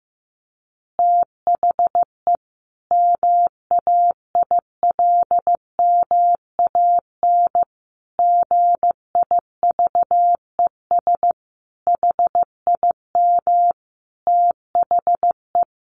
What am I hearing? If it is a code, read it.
Morse code, heard as a keyed tone, about 15 wpm, 700 Hz. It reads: THE MAILMAN GIVES HIM THE
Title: Additional Easy Phrases 15wpm